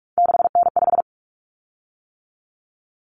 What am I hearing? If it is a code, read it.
Morse code, heard as a keyed tone, about 44 wpm, 700 Hz. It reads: 6N5
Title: Common Cut Numbers - 3-Digit 45wpm